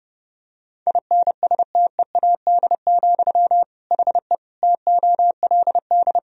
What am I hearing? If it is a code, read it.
Morse code, heard as a keyed tone, about 30 wpm, 700 Hz. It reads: INSTEAD, HE TOLD
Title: Sentences from Top 1000 Words - Review Part 4 - 30wpm